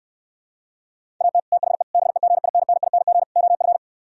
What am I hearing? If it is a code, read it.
Morse code, heard as a keyed tone, about 68 wpm, 700 Hz. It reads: AT THE BEGINNING OF